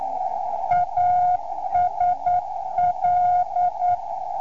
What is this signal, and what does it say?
Morse code, heard as a keyed tone, about 9 wpm, 710 Hz, with radio noise. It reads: ASL